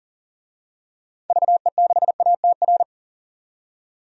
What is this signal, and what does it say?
Morse code, heard as a keyed tone, about 40 wpm, 700 Hz. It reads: VE6ATR
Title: DXCC-100 Call Signs - 6 Characters - 40wpm